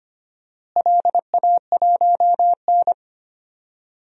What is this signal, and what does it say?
Morse code, heard as a keyed tone, about 25 wpm, 700 Hz. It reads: LA1N
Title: DXCC-100 Call Signs - 4 Characters - Repeated 2x - 25wpm